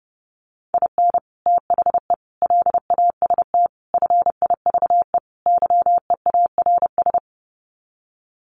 Morse code, heard as a keyed tone, about 30 wpm, 700 Hz. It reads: IN THE LAST FIVE YEARS
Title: Sets of 5 Words Encore 30wpm